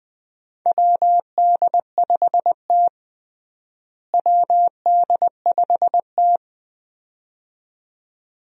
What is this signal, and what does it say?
Morse code, heard as a keyed tone, about 20 wpm, 700 Hz. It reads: WD5T WD5T
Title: USA Call Signs - 4 Characters - Repeated 2x - 20wpm